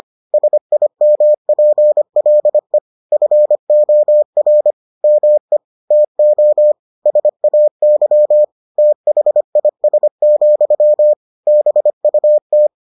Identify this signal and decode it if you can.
Morse code, heard as a keyed tone, about 25 wpm, 600 Hz. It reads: SIMPLE FOR ME TO SAY THIS, BUT